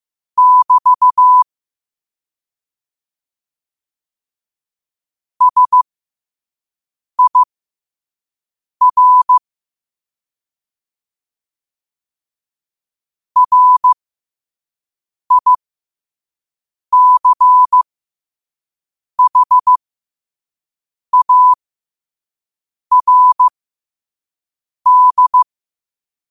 Morse code, heard as a keyed tone, about 15 wpm, spaced as for 5 wpm, 1000 Hz. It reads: <BT> SIR RICHARD
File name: qotdfiles / 5wpm